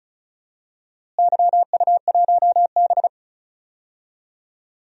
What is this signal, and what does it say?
Morse code, heard as a keyed tone, about 36 wpm, 700 Hz. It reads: YU1B
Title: DXCC-100 Call Signs - 4 Characters - Repeated 3x - 35wpm